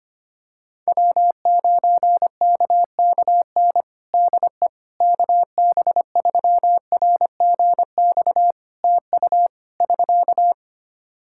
Morse code, heard as a keyed tone, about 25 wpm, 700 Hz. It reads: W9KKN DE KB3RGX TU <SK>
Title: Practice QSO - Exchange 4 - Ending - 25wpm